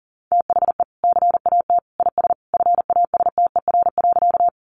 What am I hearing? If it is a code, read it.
Morse code, heard as a keyed tone, about 40 wpm, 700 Hz. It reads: THE CAT IS FASTER.
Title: Additional Easy Phrases 40wpm